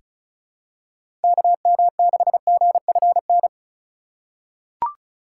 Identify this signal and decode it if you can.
Morse code, heard as a keyed tone, about 36 wpm, 700 Hz. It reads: KM6GFN
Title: USA Call Signs - 6 Characters - 3x Character Spacing - 35wpm